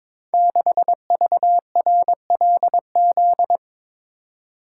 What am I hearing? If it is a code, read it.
Morse code, heard as a keyed tone, about 22 wpm, 700 Hz. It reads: 6VRLZ
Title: Code Groups - 5 Letter:Numbers - 22wpm